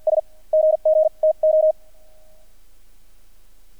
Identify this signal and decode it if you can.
Morse code, heard as a keyed tone, about 48 wpm, spaced as for 41 wpm, 630 Hz. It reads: S GG TO
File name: archived / CRSnet